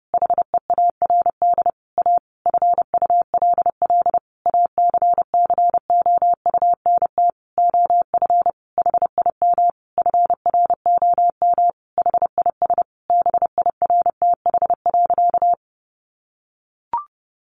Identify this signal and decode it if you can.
Morse code, heard as a keyed tone, about 30 wpm, 700 Hz. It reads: HEARD A FULL ACCOUNT OF HIM FROM HIS BIRTH.
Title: Sentences from Top 900 Words 30wpm